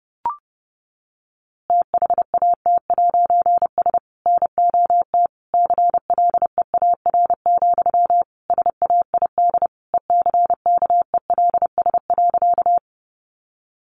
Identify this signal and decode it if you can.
Morse code, heard as a keyed tone, about 30 wpm, 700 Hz. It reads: THAT'S NOT CLEAR, SAID ECKELS.
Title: A Sound Of Thunder 30wpm